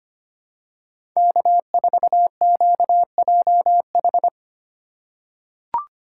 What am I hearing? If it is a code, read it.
Morse code, heard as a keyed tone, about 25 wpm, 700 Hz. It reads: K4QJH
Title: USA Call Signs - 2x Character Spacing - 25wpm